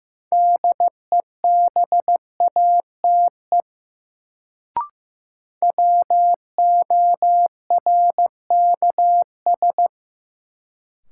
Morse code, heard as a keyed tone, about 15 wpm, 700 Hz. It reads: DEBATE WORKS
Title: Sentences from Top 1100 Words - New Word Drill Alt - 15wpm